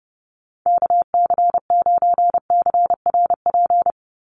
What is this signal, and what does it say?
Morse code, heard as a keyed tone, about 30 wpm, 700 Hz. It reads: KC9CRP
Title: USA Call Signs 30wpm